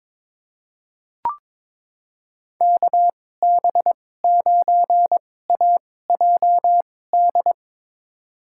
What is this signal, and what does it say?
Morse code, heard as a keyed tone, about 22 wpm, spaced as for 16 wpm, 700 Hz. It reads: KB9AJD
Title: USA Call Signs - 2x Character Spacing - 22wpm